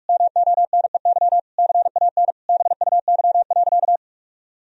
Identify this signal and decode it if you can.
Morse code, heard as a keyed tone, about 46 wpm, 700 Hz. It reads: MONEY CAN BUY.
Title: Sentences from Top 1000 Words - Review Part 4 - 45wpm